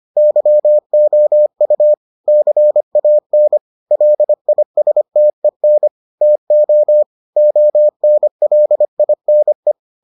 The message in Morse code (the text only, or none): YOU CAN LISTEN TO ONLINE